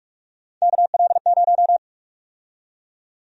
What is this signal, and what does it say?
Morse code, heard as a keyed tone, about 41 wpm, 700 Hz. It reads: KL0
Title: USA Call Sign Prefixes - Repeated 2x - 45wpm